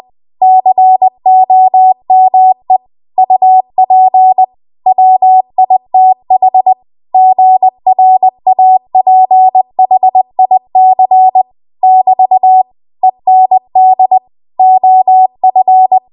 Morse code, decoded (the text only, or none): COME UP WITH GRAPHIC <BT> END OF